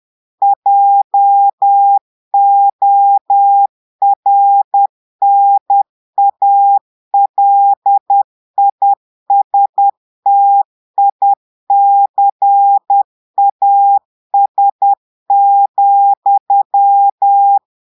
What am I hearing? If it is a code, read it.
Morse code, heard as a keyed tone, about 10 wpm, 800 Hz. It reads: JORNALISTICAS,
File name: portugais texte en clair 2 vitesse 10 mots minutes